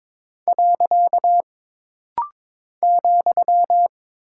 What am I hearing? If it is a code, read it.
Morse code, heard as a keyed tone, about 22 wpm, 700 Hz. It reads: . ,
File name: practice / 22wpm / Single Punctuation-Prosign - Rapid-Fire - 22wpm